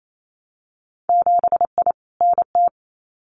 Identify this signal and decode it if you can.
Morse code, heard as a keyed tone, about 28 wpm, 700 Hz. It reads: 7I NT
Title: Winter Field Day Exchanges Encore 28wpm